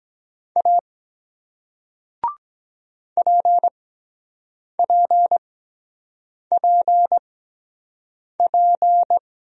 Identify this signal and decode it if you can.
Morse code, heard as a keyed tone, about 23 wpm, spaced as for 7 wpm, 700 Hz. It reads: A PPPP